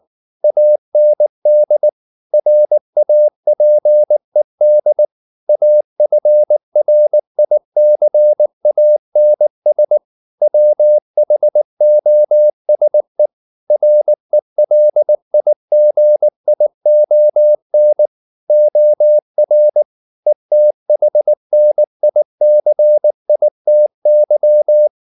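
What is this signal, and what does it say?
Morse code, heard as a keyed tone, about 19 wpm, 600 Hz. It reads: AND RAPED AFRICANS WHOSE RELIGION OR ETHNICITY
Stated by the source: Morse code at 19 wpm from CW med Gnister, reading Ken Follett's novel Never in English.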